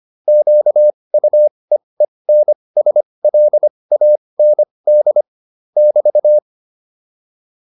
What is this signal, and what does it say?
Morse code, heard as a keyed tone, about 25 wpm, spaced as for 20 wpm, 600 Hz. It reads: QUEENSLAND <BT>
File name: QSO407 20